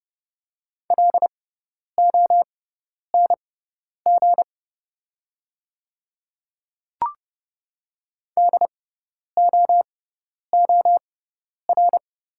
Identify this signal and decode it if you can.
Morse code, heard as a keyed tone, about 30 wpm, spaced as for 10 wpm, 700 Hz. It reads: LONG DOOR
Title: 6x Character Spacing - Top 300 Words - 30wpm